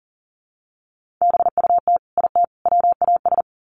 Morse code, heard as a keyed tone, about 39 wpm, 700 Hz. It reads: BUT IT WAS
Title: Sets of 3 Words 40wpm